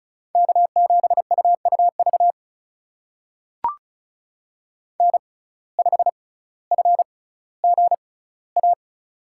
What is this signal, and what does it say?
Morse code, heard as a keyed tone, about 34 wpm, spaced as for 18 wpm, 700 Hz. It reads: K7UUV N5FGA
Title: USA Call Signs - 5 Characters - 6x Character Spacing - 35wpm